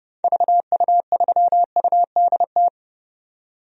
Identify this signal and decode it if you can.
Morse code, heard as a keyed tone, about 30 wpm, 700 Hz. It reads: VU3UDT
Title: DXCC-100 Call Signs - 6 Characters - 30wpm